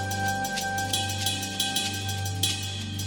There is no code.